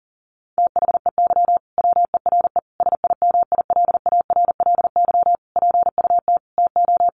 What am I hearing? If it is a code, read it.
Morse code, heard as a keyed tone, about 40 wpm, 700 Hz. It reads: THEY WERE SIMILARLY PUT TO
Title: Sentences from Top 1400 Words 40wpm